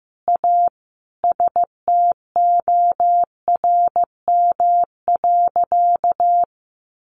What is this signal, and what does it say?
Morse code, heard as a keyed tone, about 14 wpm, 700 Hz. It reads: A STORM.
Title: Sentences from Top 1500 Words 15wpm